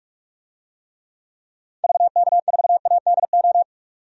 Morse code, heard as a keyed tone, about 42 wpm, 700 Hz. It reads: VK4ADO
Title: DXCC-100 Call Signs - 6 Characters - 45wpm